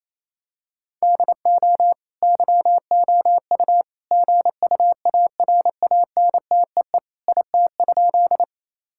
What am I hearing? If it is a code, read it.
Morse code, heard as a keyed tone, about 28 wpm, 700 Hz. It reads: DO YOU GUARANTEE IT?
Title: Sentences from Top 1600 Words 28wpm